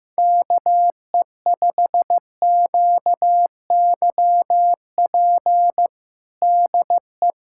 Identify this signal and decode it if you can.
Morse code, heard as a keyed tone, about 15 wpm, 700 Hz. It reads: KE5QYP DE
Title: USA Callsign DE Callsign - 6 Characters - 15wpm